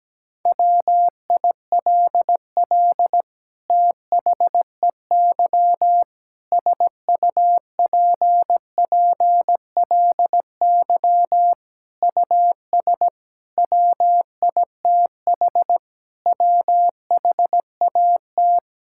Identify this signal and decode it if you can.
Morse code, heard as a keyed tone, about 17 wpm, 700 Hz. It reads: WILL THEY SUPPLY US WITH WHAT
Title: Sentences from Top 1000 Words - Review Part 5 - 17wpm